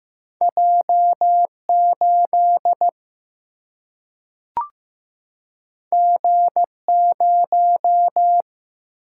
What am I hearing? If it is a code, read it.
Morse code, heard as a keyed tone, about 15 wpm, 700 Hz. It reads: J 8 G 0
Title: Single-Letter Single-Number 15wpm